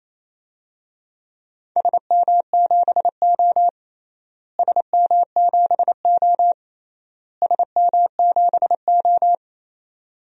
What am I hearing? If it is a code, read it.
Morse code, heard as a keyed tone, about 28 wpm, 700 Hz. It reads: SM7O SM7O SM7O
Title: DXCC-100 Call Signs - 4 Characters - Repeated 3x - 28wpm